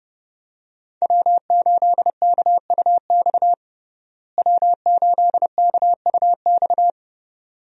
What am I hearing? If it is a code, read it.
Morse code, heard as a keyed tone, about 30 wpm, 700 Hz. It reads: W8KUX W8KUX
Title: USA Call Signs - 5 Characters - Repeated 2x - 30wpm